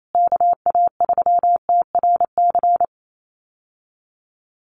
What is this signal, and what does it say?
Morse code, heard as a keyed tone, about 28 wpm, 700 Hz. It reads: KA3TRC
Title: USA Call Signs Repeated 3x 28wpm